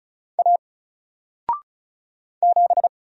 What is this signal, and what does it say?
Morse code, heard as a keyed tone, about 36 wpm, 700 Hz. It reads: A 7
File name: Single-Letter-Number-35wpm